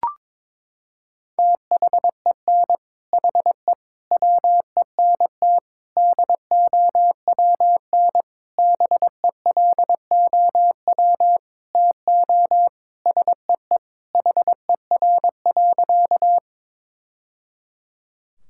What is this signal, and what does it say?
Morse code, heard as a keyed tone, about 22 wpm, 700 Hz. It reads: THEN HE WENT DOWN BELOW TO SEE HER.
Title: Sentences from Top 1000 Words - Review Part 5 - 22wpm